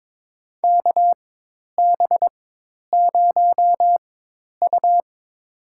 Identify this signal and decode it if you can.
Morse code, heard as a keyed tone, about 22 wpm, spaced as for 10 wpm, 700 Hz. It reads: KB0U
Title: USA Call Signs - 4 Characters - 4x Character Spacing - 22wpm